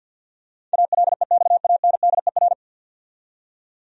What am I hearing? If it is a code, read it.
Morse code, heard as a keyed tone, about 47 wpm, 700 Hz. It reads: ALEXANDER
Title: USA Common Names - 50wpm